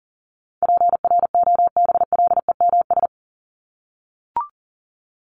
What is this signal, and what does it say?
Morse code, heard as a keyed tone, about 37 wpm, 700 Hz. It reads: PROBLEMS
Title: ES:S Suffix - Top 100 Words with Base Words 40wpm